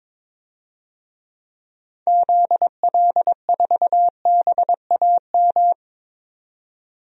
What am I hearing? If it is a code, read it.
Morse code, heard as a keyed tone, about 22 wpm, 700 Hz. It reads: ZL4BAM
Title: DXCC-100 Call Signs - 6 Characters - 22wpm